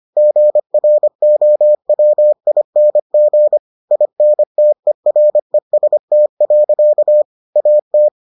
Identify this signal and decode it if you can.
Morse code, heard as a keyed tone, about 25 wpm, 600 Hz. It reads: GROWING INTEREST. AT